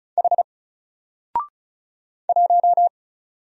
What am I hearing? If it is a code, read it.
Morse code, heard as a keyed tone, about 34 wpm, 700 Hz. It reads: H 1
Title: Single Letter-Number Alt 35wpm